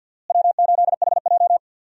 Morse code, heard as a keyed tone, about 47 wpm, 700 Hz. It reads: W8HJ
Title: USA Call Signs - 4 Characters - Repeated 2x - 50wpm